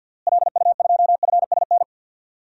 Morse code, heard as a keyed tone, about 51 wpm, 700 Hz. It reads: LU2FSN
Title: DXCC-100 Call Signs - 6 Characters - Repeated 3x - 50wpm